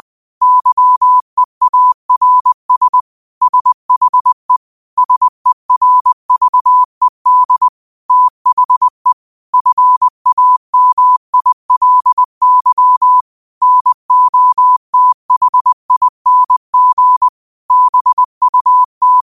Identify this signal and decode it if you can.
Morse code, heard as a keyed tone, about 20 wpm, 1000 Hz. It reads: YEARS SHE SERVED THE FAMILY NOTHING BUT
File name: qotdfiles / 20wpm